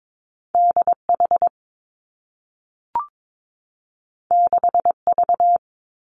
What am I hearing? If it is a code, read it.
Morse code, heard as a keyed tone, about 22 wpm, 700 Hz. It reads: D H 6 V
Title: Two Challenging Letter-Number Combinations 22wpm